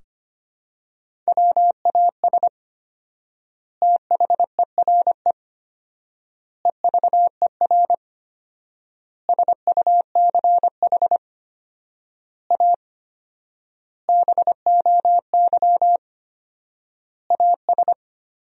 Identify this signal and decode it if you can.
Morse code, heard as a keyed tone, about 25 wpm, 700 Hz. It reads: WAS THERE EVER SUCH A BOY AS
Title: 4x Word-Spacing - Sentences from Top 400 Words - 25wpm